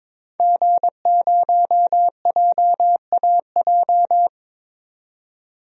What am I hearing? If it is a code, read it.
Morse code, heard as a keyed tone, about 22 wpm, 700 Hz. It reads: G0JAJ
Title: DXCC-100 Call Signs - 5 Characters - 22wpm